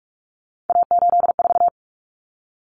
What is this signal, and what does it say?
Morse code, heard as a keyed tone, about 44 wpm, 700 Hz. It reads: A84
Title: Common Cut Numbers - 3-Digit 45wpm